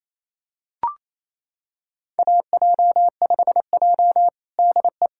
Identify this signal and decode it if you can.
Morse code, heard as a keyed tone, about 28 wpm, 700 Hz. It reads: AJ5J DE
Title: USA Callsign DE Callsign - 4 Characters - 28wpm